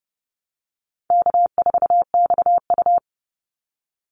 Morse code, heard as a keyed tone, about 30 wpm, 700 Hz. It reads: K4XU
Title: Easy Call Signs - 30wpm